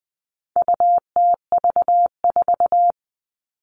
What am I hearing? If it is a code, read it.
Morse code, heard as a keyed tone, about 20 wpm, 700 Hz. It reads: UTV4
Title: All Cut Numbers - 4-Digit 20wpm